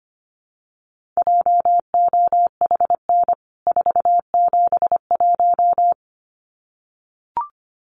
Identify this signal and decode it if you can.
Morse code, heard as a keyed tone, about 25 wpm, 690 Hz. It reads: JOHN 471
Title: CWT Exchange 25wpm